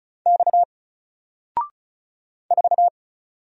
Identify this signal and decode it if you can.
Morse code, heard as a keyed tone, about 36 wpm, 700 Hz. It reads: X 4
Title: Single Letter-Number 35wpm